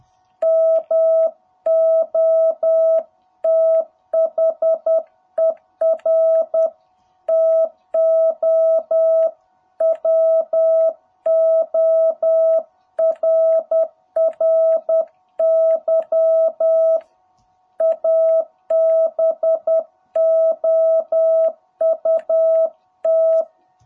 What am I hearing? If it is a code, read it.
Morse code, heard as a keyed tone, about 10 wpm, 660 Hz. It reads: MOTHER TO WORRY ABOUT
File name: jh8cbh / 2021cw